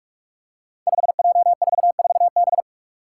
Morse code, heard as a keyed tone, about 42 wpm, 700 Hz. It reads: 5J44B
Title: Five Challenging Letter-Number Combinations 45wpm